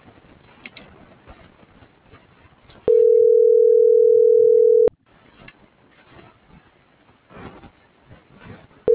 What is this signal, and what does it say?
Ringback tone (North America), 440 + 480 Hz.
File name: CENTRAL GENERAL / out daniel.valazco